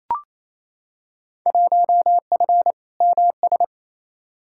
Morse code, heard as a keyed tone, about 28 wpm, 700 Hz. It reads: 1F MS
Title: Field Day Exchanges Encore - 28wpm